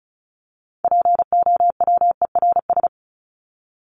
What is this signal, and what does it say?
Morse code, heard as a keyed tone, about 34 wpm, 700 Hz. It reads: POWERS
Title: ES:S Suffix - Top 100 Words with Base Words 35wpm